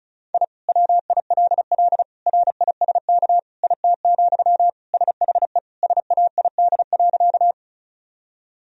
Morse code, heard as a keyed tone, about 36 wpm, 700 Hz. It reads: I WILL RISK IT, SHE SAID.
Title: Sentences from Top 1000 Words - Review Part 4 - 35wpm